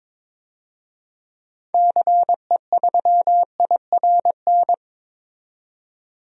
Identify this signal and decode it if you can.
Morse code, heard as a keyed tone, about 22 wpm, 700 Hz. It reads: CE3IRN
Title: DXCC-100 Call Signs - 6 Characters - 22wpm